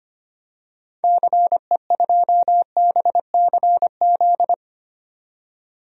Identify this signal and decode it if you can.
Morse code, heard as a keyed tone, about 25 wpm, 700 Hz. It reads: CE2BCZ
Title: DXCC-100 Call Signs - 6 Characters - 25wpm